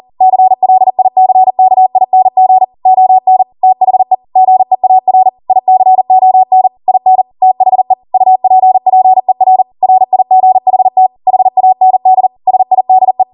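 Morse code, heard as a keyed tone, about 40 wpm, 750 Hz. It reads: CLICKING ON THE GEAR ICON IN THE UPPER RIGHT HAND SIDE
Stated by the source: ARRL 40 WPM Morse code practice transmission.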